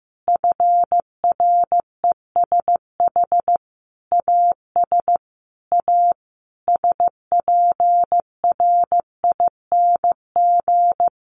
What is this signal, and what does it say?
Morse code, heard as a keyed tone, about 15 wpm, 700 Hz. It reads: FRESH AS A SPRING
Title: Sentences from Top 1200 Words 15wpm